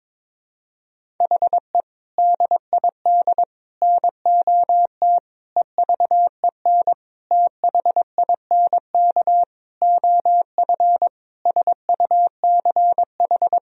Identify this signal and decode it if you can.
Morse code, heard as a keyed tone, about 22 wpm, 700 Hz. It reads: HE DID NOT EVEN THINK OF SUCH
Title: Sentences from Top 1000 Words - Review Part 1 - 22wpm